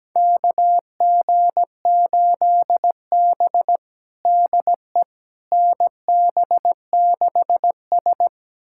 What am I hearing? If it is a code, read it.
Morse code, heard as a keyed tone, about 17 wpm, 700 Hz. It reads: KG8B DE NB6S
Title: USA Callsign DE Callsign - 4 Characters - 17wpm